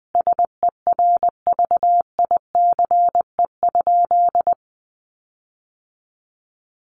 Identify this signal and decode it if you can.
Morse code, heard as a keyed tone, about 20 wpm, 700 Hz. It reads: SERVICE?
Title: Sentences from Top 1500 Words 20wpm